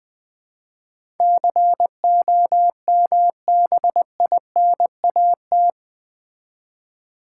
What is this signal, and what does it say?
Morse code, heard as a keyed tone, about 20 wpm, 700 Hz. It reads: COMBINAT
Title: Build A Word - 4 Letters At A Time - 11-Letter Words - 20wpm